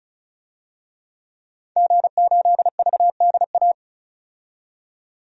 Morse code, heard as a keyed tone, about 36 wpm, 700 Hz. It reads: G8VDA
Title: DXCC-100 Call Signs - 5 Characters - 35wpm